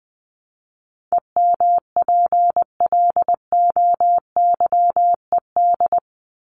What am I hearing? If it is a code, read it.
Morse code, heard as a keyed tone, about 20 wpm, 700 Hz. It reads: EMPLOYED
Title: ED Suffix - Top 100 Words with Base Words 20wpm